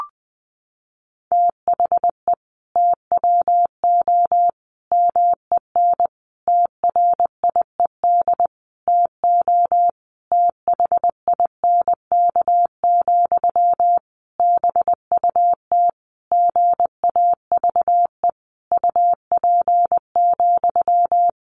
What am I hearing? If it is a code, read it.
Morse code, heard as a keyed tone, about 20 wpm, 700 Hz. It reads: THE TWO MEN TRIED TO THINK, BUT GAVE UP,
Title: A Sound Of Thunder 20wpm